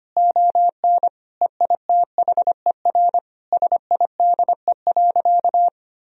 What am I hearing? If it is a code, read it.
Morse code, heard as a keyed tone, about 25 wpm, 700 Hz. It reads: ON EITHER SIDE.
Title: Sentences from Top 1000 Words - Review Part 4 - 25wpm